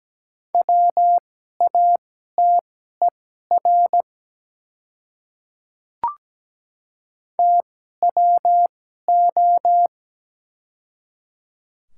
Morse code, heard as a keyed tone, about 17 wpm, spaced as for 12 wpm, 690 Hz. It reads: WATER TWO
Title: 2x Character Spacing - Top 100 Words - 17wpm